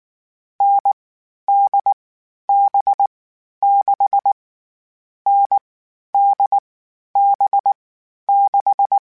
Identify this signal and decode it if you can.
Morse code, heard as a keyed tone, about 19 wpm, 800 Hz. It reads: N D B 6 N D B 6
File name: Variable Speed